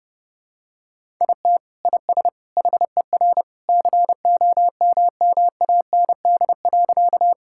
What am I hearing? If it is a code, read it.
Morse code, heard as a keyed tone, about 30 wpm, 700 Hz. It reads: IT IS HER COMMAND.
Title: Sentences from Top 1600 Words 30wpm